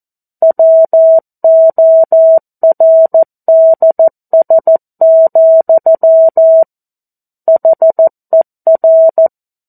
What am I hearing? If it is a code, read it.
Morse code, heard as a keyed tone, about 14 wpm, 650 Hz. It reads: WORDS, HER
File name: 02-Chapter-02 14 WPM